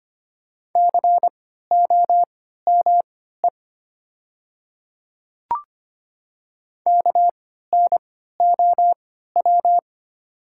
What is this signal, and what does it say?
Morse code, heard as a keyed tone, about 25 wpm, spaced as for 14 wpm, 700 Hz. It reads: COME KNOW
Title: 3x Character Spacing - Top 100 Words - 25wpm